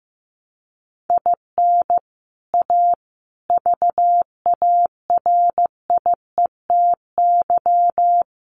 Morse code, heard as a keyed tone, about 15 wpm, 700 Hz. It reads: IN A VARIETY
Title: Sets of 5 Words Encore 15wpm